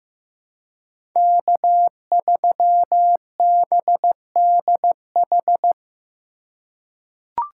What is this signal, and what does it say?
Morse code, heard as a keyed tone, about 15 wpm, 700 Hz. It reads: K3BDH
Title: USA Call Signs - 5 Characters - 3x Character Spacing - 15wpm